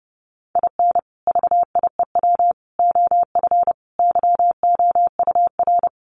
Morse code, heard as a keyed tone, about 30 wpm, 700 Hz. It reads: IN VIEW OF YOUR
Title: Sentences from Top 900 Words 30wpm